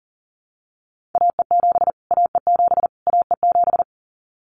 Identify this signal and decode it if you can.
Morse code, heard as a keyed tone, about 40 wpm, 700 Hz. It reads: AE7 AE7 AE7
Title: Callsign Prefixes for Top 25 DXCC - Repeated 3x - 40wpm